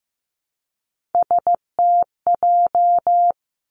Morse code, heard as a keyed tone, about 15 wpm, 700 Hz. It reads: STJ
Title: Three Challenging Letter-Number Combinations 15wpm